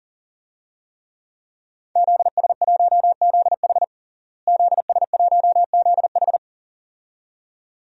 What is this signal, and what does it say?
Morse code, heard as a keyed tone, about 37 wpm, 700 Hz. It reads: ZS1ZH ZS1ZH
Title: DXCC-100 Call Signs - 5 Characters - Repeated 2x - 40wpm